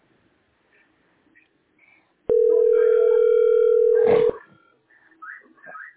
Ringback tone (North America), 440 + 480 Hz.